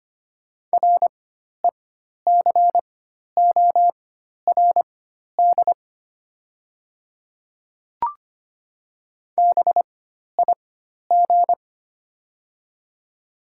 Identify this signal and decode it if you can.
Morse code, heard as a keyed tone, about 25 wpm, spaced as for 12 wpm, 700 Hz. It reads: RECORD BIG
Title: 4x Character Spacing - Top 500 Words - 25wpm